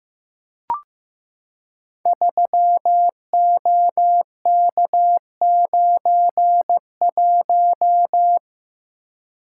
Morse code, heard as a keyed tone, about 15 wpm, 700 Hz. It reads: 3OK91
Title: Code Groups - 5 Letter:Numbers - 15wpm